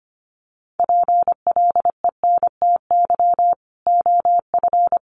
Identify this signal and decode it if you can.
Morse code, heard as a keyed tone, about 25 wpm, 700 Hz. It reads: PLENTY OF
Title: Sentences from Top 1500 Words 25wpm